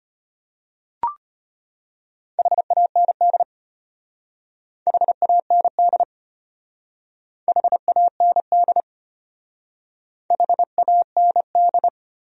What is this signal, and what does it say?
Morse code, heard as a keyed tone, about 31 wpm, 700 Hz. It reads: HAND HAND HAND HAND
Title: Speed-Racing Top 200 Words 25wpm